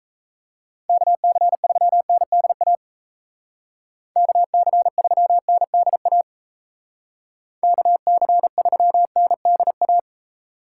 Morse code, heard as a keyed tone, about 34 wpm, 700 Hz. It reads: KC3NDA KC3NDA KC3NDA
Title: Speed Racing - USA 2x3 Call Signs 28wpm